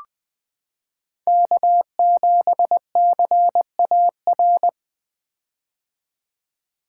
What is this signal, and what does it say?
Morse code, heard as a keyed tone, about 20 wpm, 700 Hz. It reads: K7CAR
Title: 2023 POTA Activator Call Signs - 20wpm